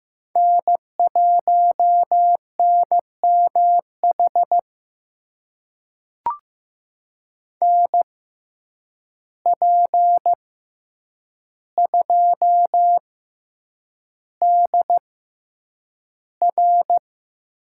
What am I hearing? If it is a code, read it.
Morse code, heard as a keyed tone, about 15 wpm, spaced as for 8 wpm, 700 Hz. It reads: N1NMH NP2DR
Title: USA Call Signs - 5 Characters - 6x Character Spacing - 15wpm